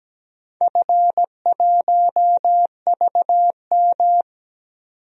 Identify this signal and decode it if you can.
Morse code, heard as a keyed tone, about 17 wpm, 700 Hz. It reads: F1VM
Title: DXCC-100 Call Signs - 4 Characters - 17wpm